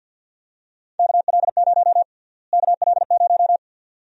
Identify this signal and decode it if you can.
Morse code, heard as a keyed tone, about 52 wpm, 700 Hz. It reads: KL0 KL0
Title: USA Call Sign Prefixes - Repeated 2x - 50wpm